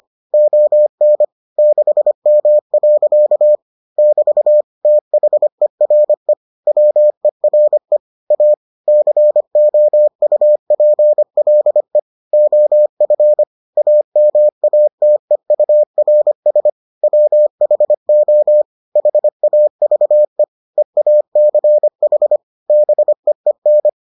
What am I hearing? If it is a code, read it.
Morse code, heard as a keyed tone, about 25 wpm, 600 Hz. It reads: ON 6M. <BT> THERE WERE A COUPLE OF AMATEURS WHO HAVE EACH BEEN